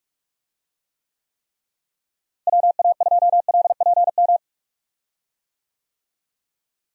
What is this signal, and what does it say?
Morse code, heard as a keyed tone, about 42 wpm, 700 Hz. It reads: WA2LPM
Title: USA Call Signs - 6 Characters - Repeated 3x - 45wpm